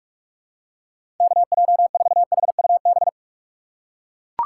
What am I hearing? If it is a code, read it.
Morse code, heard as a keyed tone, about 43 wpm, 700 Hz. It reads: KJ4HUB
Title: USA Call Signs - 6 Characters - 6x Character Spacing - 45wpm